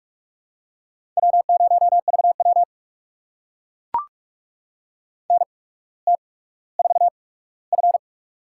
Morse code, heard as a keyed tone, about 44 wpm, spaced as for 19 wpm, 700 Hz. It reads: W0VW NT4F
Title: USA Call Signs - 4 Characters - 8x Character Spacing - 45wpm